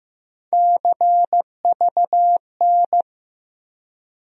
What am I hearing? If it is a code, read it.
Morse code, heard as a keyed tone, about 15 wpm, 700 Hz. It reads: CVN
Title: USA Call Sign Postfixes - 3 Letters 15wpm